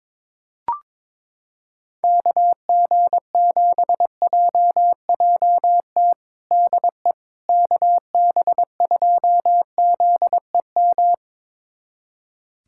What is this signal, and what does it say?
Morse code, heard as a keyed tone, about 22 wpm, 700 Hz. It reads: KG7JJT DE KB2ZEM
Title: USA Callsign DE Callsign - 6 Characters - 22wpm